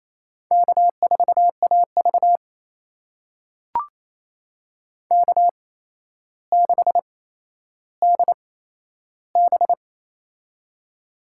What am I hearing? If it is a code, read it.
Morse code, heard as a keyed tone, about 28 wpm, spaced as for 12 wpm, 700 Hz. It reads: K4AV K6DB
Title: USA Call Signs - 4 Characters - 8x Character Spacing - 28wpm